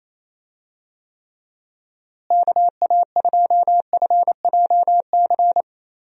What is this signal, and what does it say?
Morse code, heard as a keyed tone, about 28 wpm, 700 Hz. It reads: KA2FJC
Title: USA Call Signs - 6 Characters - Repeated 2x - 28wpm